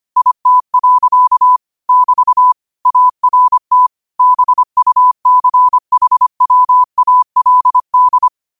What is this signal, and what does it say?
Morse code, heard as a keyed tone, about 25 wpm, 1000 Hz. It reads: IT. <BT> ART BUCHWALD